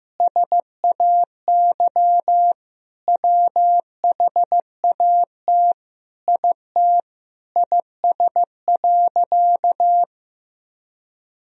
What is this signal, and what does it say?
Morse code, heard as a keyed tone, about 15 wpm, 700 Hz. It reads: SAY WHAT IT IS.
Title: Sentences from Top 200 Words and Abbreviations 15wpm